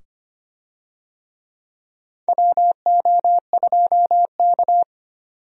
Morse code, heard as a keyed tone, about 25 wpm, 700 Hz. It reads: WO2K
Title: USA Call Signs - 4 Characters - 25wpm